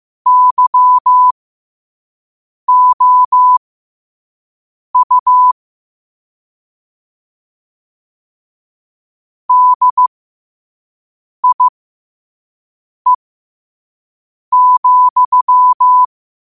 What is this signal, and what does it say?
Morse code, heard as a keyed tone, about 15 wpm, spaced as for 5 wpm, 1000 Hz. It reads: YOU DIE,